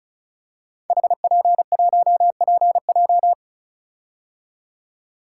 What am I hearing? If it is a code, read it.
Morse code, heard as a keyed tone, about 36 wpm, 700 Hz. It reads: HP1PJ
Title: Code Groups - 5 Letter:Numbers - 35wpm